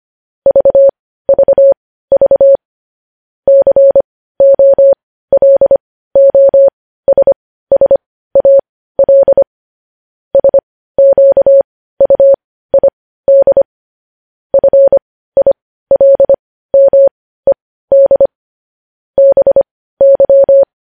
Morse code, heard as a keyed tone, about 25 wpm, spaced as for 15 wpm, 550 Hz. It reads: VVV COLOSSAL SQUID FILMED BY
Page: news headlines in morse code at 15 wpm